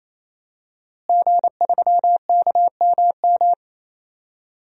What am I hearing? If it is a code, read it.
Morse code, heard as a keyed tone, about 28 wpm, 700 Hz. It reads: G3KMM
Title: DXCC-100 Call Signs - 5 Characters - Repeated 3x - 28wpm